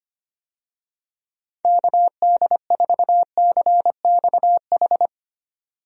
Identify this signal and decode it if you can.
Morse code, heard as a keyed tone, about 25 wpm, 700 Hz. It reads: KD4CXH
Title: USA Call Signs - 6 Characters - 25wpm